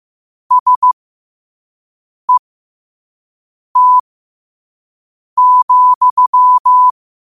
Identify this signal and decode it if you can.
Morse code, heard as a keyed tone, about 15 wpm, spaced as for 5 wpm, 1000 Hz. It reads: SET,
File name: qotdfiles / 5wpm